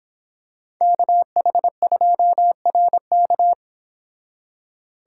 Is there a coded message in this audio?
Yes: Morse code from 0.8 to 3.5 s, heard as a keyed tone, about 26 wpm, 700 Hz. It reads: KH2RK